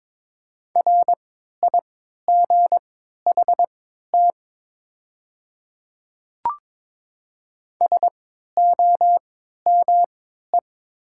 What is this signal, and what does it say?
Morse code, heard as a keyed tone, about 22 wpm, spaced as for 12 wpm, 690 Hz. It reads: RIGHT SOME
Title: 3x Character Spacing - Top 300 Words - 22wpm